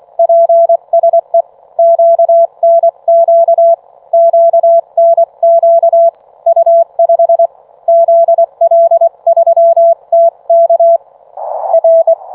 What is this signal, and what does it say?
Morse code, heard as a keyed tone, about 24 wpm, 670 Hz. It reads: PSE QNQ QNQ U5 ZL3TK R